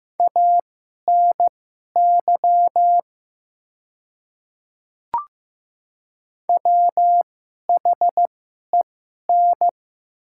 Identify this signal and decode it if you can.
Morse code, heard as a keyed tone, about 15 wpm, spaced as for 11 wpm, 700 Hz. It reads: ANY WHEN
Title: 2x Character Spacing - Top 200 Words - 15wpm